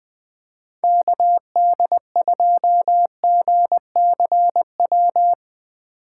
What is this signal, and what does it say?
Morse code, heard as a keyed tone, about 20 wpm, 700 Hz. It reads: KD2GCW
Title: USA Call Signs - 6 Characters - 6x Character Spacing - 20wpm